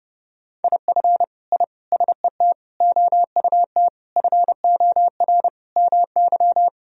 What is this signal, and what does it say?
Morse code, heard as a keyed tone, about 29 wpm, 700 Hz. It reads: IF I SET OUT FOR MY
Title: Sentences from Top 1000 Words - Review Part 2 - 30wpm